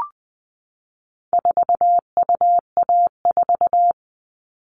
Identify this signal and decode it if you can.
Morse code, heard as a keyed tone, about 20 wpm, 700 Hz. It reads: 4UA4
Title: All Cut Numbers - 4-Digit 20wpm